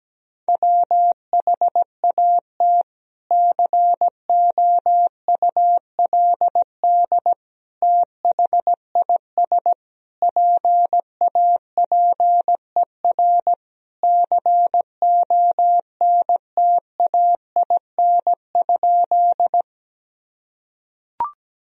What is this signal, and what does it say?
Morse code, heard as a keyed tone, about 17 wpm, 700 Hz. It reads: WHAT COULD THIS PAPER CONTAIN?
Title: Sentences from Top 1000 Words - Review Part 2 - 17wpm